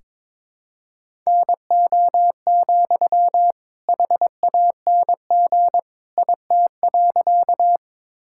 Morse code, heard as a keyed tone, about 22 wpm, 700 Hz. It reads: NO, HANG IT.
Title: Sentences from Top 1600 Words 22wpm